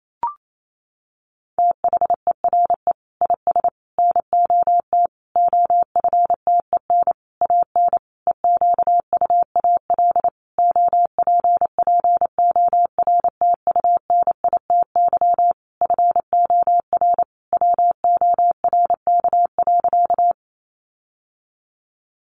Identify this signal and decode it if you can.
Morse code, heard as a keyed tone, about 28 wpm, 700 Hz. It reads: THERE IS NOT OFTEN AN EQUAL OPPORTUNITY FOR WORK.
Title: Sentences from Top 1400 Words 28wpm